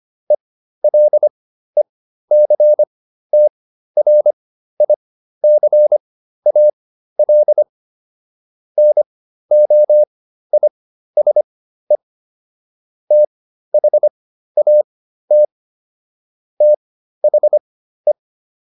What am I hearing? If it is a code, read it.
Morse code, heard as a keyed tone, about 25 wpm, spaced as for 13 wpm, 600 Hz. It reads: ELECTRICAL NOISE THAT THE